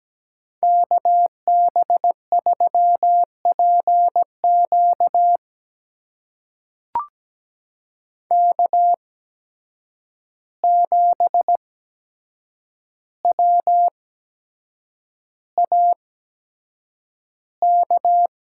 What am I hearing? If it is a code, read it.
Morse code, heard as a keyed tone, about 17 wpm, spaced as for 7 wpm, 700 Hz. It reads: KB3PQ K7WAK
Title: USA Call Signs - 5 Characters - 8x Character Spacing - 17wpm